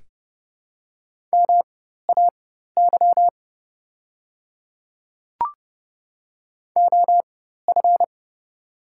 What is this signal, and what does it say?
Morse code, heard as a keyed tone, about 30 wpm, spaced as for 14 wpm, 700 Hz. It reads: MAY OF
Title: 4x Character Spacing - Top 100 Words - 30wpm